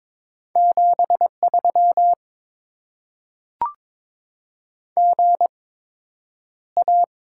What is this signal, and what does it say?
Morse code, heard as a keyed tone, about 23 wpm, spaced as for 9 wpm, 700 Hz. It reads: 73 GA
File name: Top 20 QSO Elements - 8x Character Spacing - 22wpm